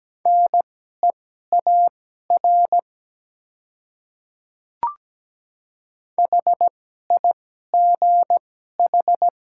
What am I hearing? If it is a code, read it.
Morse code, heard as a keyed tone, about 17 wpm, spaced as for 12 wpm, 700 Hz. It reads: NEAR HIGH
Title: 2x Character Spacing - Top 300 Words - 17wpm